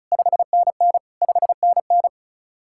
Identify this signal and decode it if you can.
Morse code, heard as a keyed tone, about 43 wpm, 700 Hz. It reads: 5NN 5NN
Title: Speed-Racing Common QSO Phrases Encore - 35wpm